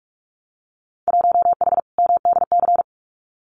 Morse code, heard as a keyed tone, about 46 wpm, 700 Hz. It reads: 1H MDC
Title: Winter Field Day Exchanges 45wpm